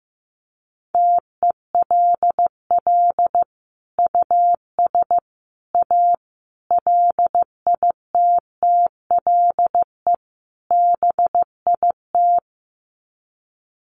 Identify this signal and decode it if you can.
Morse code, heard as a keyed tone, about 15 wpm, 700 Hz. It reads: TELL US A LITTLE BIT
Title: Sets of 5 Words Encore2 15wpm